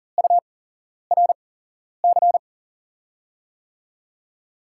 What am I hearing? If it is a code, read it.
Morse code, heard as a keyed tone, about 41 wpm, spaced as for 11 wpm, 700 Hz. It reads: URC
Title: USA Call Signs - 6 Characters - 8x Character Spacing - 40wpm